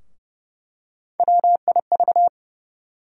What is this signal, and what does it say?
Morse code, heard as a keyed tone, about 30 wpm, 700 Hz. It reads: WIV
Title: USA Call Sign Postfixes - 3 Letters 30wpm